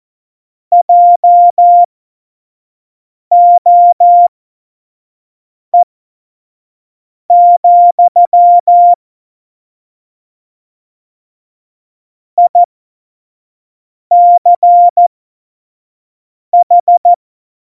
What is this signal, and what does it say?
Morse code, heard as a keyed tone, about 14 wpm, spaced as for 5 wpm, 700 Hz. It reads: JOE, ICH
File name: Morsekurs / Witze / Witz 26. Dezember 2017, 25 F70